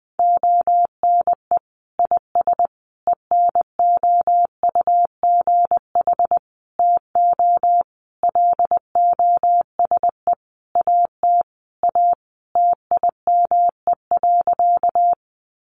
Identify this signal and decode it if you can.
Morse code, heard as a keyed tone, about 20 wpm, 700 Hz. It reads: ONE IS ENOUGH TO LOSE AT A TIME.
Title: Sentences from Top 1100 Words 20wpm